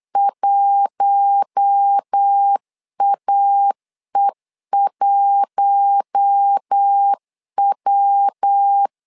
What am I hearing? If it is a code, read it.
Morse code, heard as a keyed tone, about 8 wpm, 800 Hz. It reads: 1AE1W
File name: LEZIONI / 4 Q1 8wpm 3035